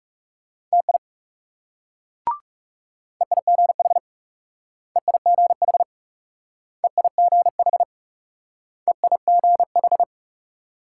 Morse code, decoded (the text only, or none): TI EIGH EIGH EIGH EIGH